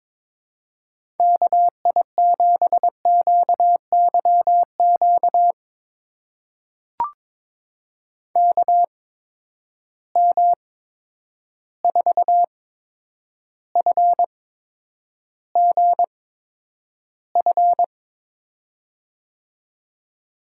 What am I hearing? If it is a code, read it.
Morse code, heard as a keyed tone, about 22 wpm, spaced as for 10 wpm, 700 Hz. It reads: KI7QYQ KM4FGF
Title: USA Call Signs - 6 Characters - 8x Character Spacing - 22wpm